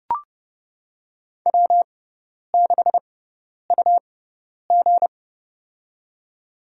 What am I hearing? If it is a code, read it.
Morse code, heard as a keyed tone, about 29 wpm, spaced as for 10 wpm, 700 Hz. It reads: W6UG
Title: USA Call Signs - 4 Characters - 6x Character Spacing - 30wpm